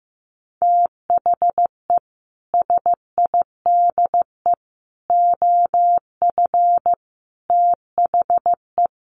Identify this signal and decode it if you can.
Morse code, heard as a keyed tone, about 15 wpm, 700 Hz. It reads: THE SIDE OF THE
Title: Sets of 5 Words Encore2 15wpm